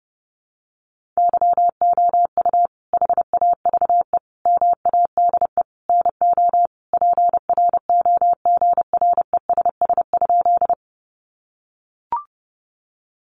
Morse code, heard as a keyed tone, about 30 wpm, 700 Hz. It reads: YOU HAVE MADE NO PROGRESS?
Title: Sentences from Top 1400 Words 30wpm